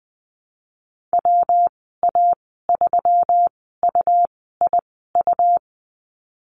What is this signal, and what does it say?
Morse code, heard as a keyed tone, about 20 wpm, spaced as for 15 wpm, 700 Hz. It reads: WA3UIU
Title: 167 Lesson 27g - Number 7 - Call Signs 20wpm